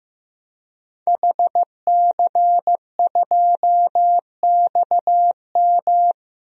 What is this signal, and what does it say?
Morse code, heard as a keyed tone, about 16 wpm, 700 Hz. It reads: HC2XM
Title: DXCC-100 Call Signs - 5 Characters - Repeated 2x - 15wpm